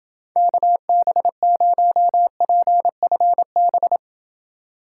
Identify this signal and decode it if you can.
Morse code, heard as a keyed tone, about 28 wpm, 700 Hz. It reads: KB0PFB